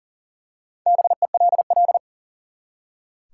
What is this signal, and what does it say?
Morse code, heard as a keyed tone, about 37 wpm, 700 Hz. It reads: BELL
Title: Four Letter Words - 40wpm